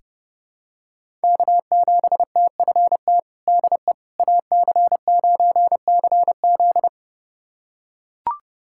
Morse code, heard as a keyed tone, about 30 wpm, 700 Hz. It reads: K7TFT DE AC9CZ
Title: USA Callsign DE Callsign - 5 Characters - 30wpm